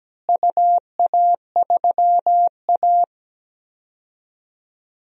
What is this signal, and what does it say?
Morse code, heard as a keyed tone, about 17 wpm, 700 Hz. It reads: UA3A
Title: DXCC-100 Call Signs - 4 Characters - Repeated 2x - 17wpm